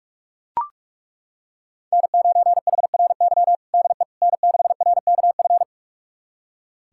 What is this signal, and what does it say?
Morse code, heard as a keyed tone, about 41 wpm, 700 Hz. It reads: N9HRY DE N6RKF
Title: USA Callsign DE Callsign - 5 Characters - 45wpm